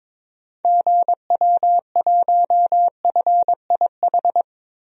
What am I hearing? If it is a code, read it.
Morse code, heard as a keyed tone, about 22 wpm, 700 Hz. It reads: GW1FIH
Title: DXCC-100 Call Signs - 6 Characters - Repeated 3x - 22wpm